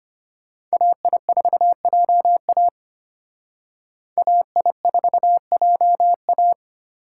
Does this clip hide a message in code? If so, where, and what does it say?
Yes: Morse code from 0.7 to 6.5 s, heard as a keyed tone, about 32 wpm, 700 Hz. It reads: AI4JA AI4JA